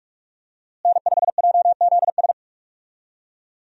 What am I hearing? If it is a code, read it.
Morse code, heard as a keyed tone, about 43 wpm, 700 Hz. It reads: N5JZS
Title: USA Call Signs - 5 Characters - Repeated 2x - 45wpm